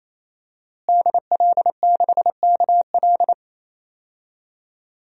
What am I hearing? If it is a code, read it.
Morse code, heard as a keyed tone, about 27 wpm, 700 Hz. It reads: DL6KL
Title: DXCC-100 Call Signs - 5 Characters - Repeated 3x - 28wpm